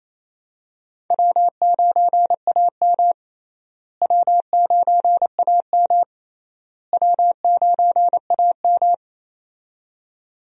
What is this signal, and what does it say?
Morse code, heard as a keyed tone, about 28 wpm, 700 Hz. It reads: W9AM W9AM W9AM
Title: USA Call Signs - 4 Characters - Repeated 3x - 28wpm